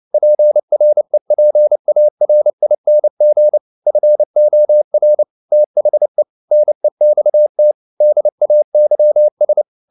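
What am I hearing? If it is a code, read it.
Morse code, heard as a keyed tone, about 29 wpm, 600 Hz. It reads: PREPARING FOR THE NEXT DAYS
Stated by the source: Morse code at 29 wpm, sending Ken Follett's novel Never in English, from CW med Gnister.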